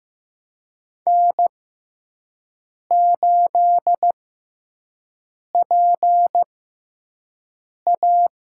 Morse code, heard as a keyed tone, about 15 wpm, spaced as for 5 wpm, 700 Hz. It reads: N8PA
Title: USA Call Signs - 5 Characters - 6x Character Spacing - 15wpm